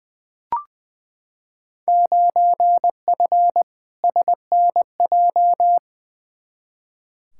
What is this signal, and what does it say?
Morse code, heard as a keyed tone, about 20 wpm, 700 Hz. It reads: 9F SNJ
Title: Field Day Exchanges Encore - 20wpm